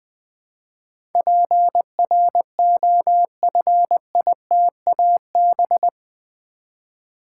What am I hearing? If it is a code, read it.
Morse code, heard as a keyed tone, about 20 wpm, 700 Hz. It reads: PROFITAB
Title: Build A Word - 4 Letters At A Time - 10-Letter Words - 20wpm